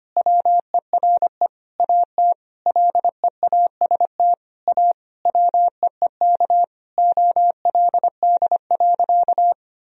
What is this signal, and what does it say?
Morse code, heard as a keyed tone, about 25 wpm, 700 Hz. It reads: WERE AT LEAST A WEEK OLD.
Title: Sentences from Top 1000 Words - Review Part 2 - 25wpm